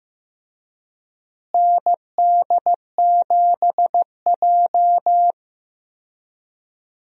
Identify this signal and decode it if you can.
Morse code, heard as a keyed tone, about 15 wpm, 700 Hz. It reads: ND7J
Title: USA Call Signs - 4 Characters - Repeated 3x - 15wpm